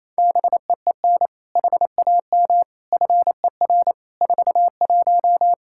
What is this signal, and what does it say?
Morse code, heard as a keyed tone, about 28 wpm, 700 Hz. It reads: BEEN HAM FER 41
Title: Practice QSO - Exchange 3 - More Info - 28wpm